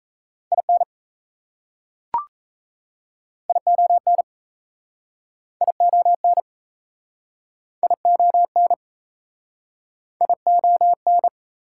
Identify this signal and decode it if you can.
Morse code, heard as a keyed tone, about 38 wpm, 700 Hz. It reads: IN ION ION ION ION